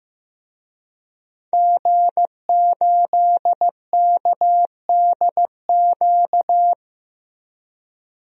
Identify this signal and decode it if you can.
Morse code, heard as a keyed tone, about 15 wpm, 700 Hz. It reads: G8KDQ
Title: DXCC-100 Call Signs - 5 Characters - Repeated 2x - 15wpm